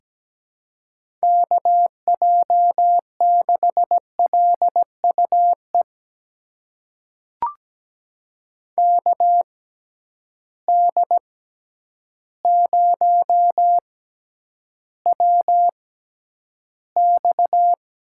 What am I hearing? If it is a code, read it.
Morse code, heard as a keyed tone, about 17 wpm, spaced as for 9 wpm, 700 Hz. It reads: KJ6LUE KD0WX
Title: USA Call Signs - 6 Characters - 6x Character Spacing - 17wpm